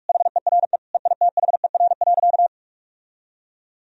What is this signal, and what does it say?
Morse code, heard as a keyed tone, about 46 wpm, 700 Hz. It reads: HERE EITHER.
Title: Sentences from Top 1000 Words - Review Part 4 - 45wpm